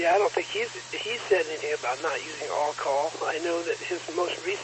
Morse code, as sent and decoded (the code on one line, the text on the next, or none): none